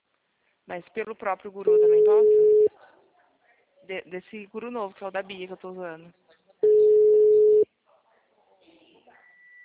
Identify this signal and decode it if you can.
Ringback tone (Europe, 425 Hz), with speech.
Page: Marilia